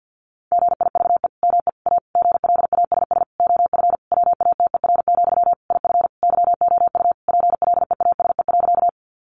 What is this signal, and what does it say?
Morse code, heard as a keyed tone, about 50 wpm, 700 Hz. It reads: GIVE ME A GLASS OF WATER, IF YOU PLEASE.
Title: Sentences from Top 1500 Words 50wpm